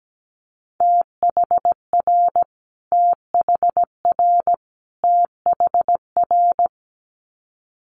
Morse code, heard as a keyed tone, about 17 wpm, 700 Hz. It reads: THR THR THR
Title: Multi-letter Phonemes Repeated 3x 17wpm